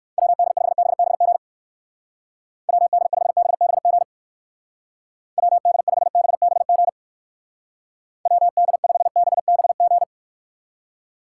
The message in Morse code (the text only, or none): WD5BBG WD5BBG WD5BBG WD5BBG